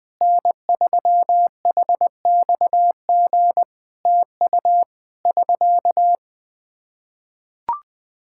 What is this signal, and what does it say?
Morse code, heard as a keyed tone, about 20 wpm, 700 Hz. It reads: N3HXG TU <SK>
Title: Practice QSO - Exchange 4 - Ending - 20wpm